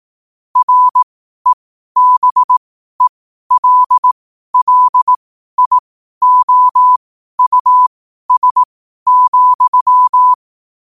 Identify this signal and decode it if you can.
Morse code, heard as a keyed tone, about 18 wpm, spaced as for 12 wpm, 1000 Hz. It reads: REBELLIOUS,
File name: qotdfiles / 12wpm